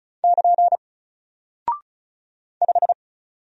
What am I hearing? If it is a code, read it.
Morse code, heard as a keyed tone, about 35 wpm, 700 Hz. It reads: <KN> 5
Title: Single Character (Letter:Number:Punctuation:Prosign) 35wpm